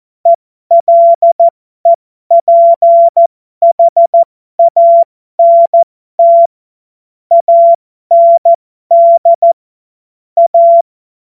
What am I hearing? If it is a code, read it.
Morse code, heard as a keyed tone, about 14 wpm, spaced as for 12 wpm, 680 Hz. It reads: ELEPHANT AND A
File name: Morsekurs / Witze